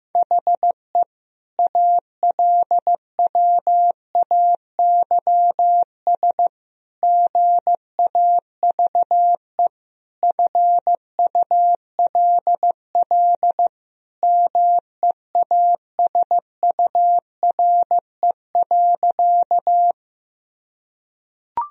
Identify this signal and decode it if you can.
Morse code, heard as a keyed tone, about 15 wpm, 700 Hz. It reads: HE ALWAYS GAVE FULL MEASURE.
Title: Sentences from Top 1000 Words - Review Part 3 - 15wpm